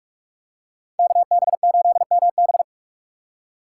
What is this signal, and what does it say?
Morse code, heard as a keyed tone, about 48 wpm, 700 Hz. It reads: KB8MB